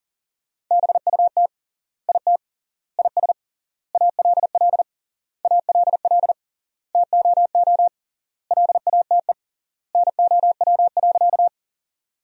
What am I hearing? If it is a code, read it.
Morse code, heard as a keyed tone, about 40 wpm, 700 Hz. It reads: BUT IT IS ALL ALL TOO LATE NOW.
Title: 3x Word-Spacing - Sentences from Top 300 Words - 40wpm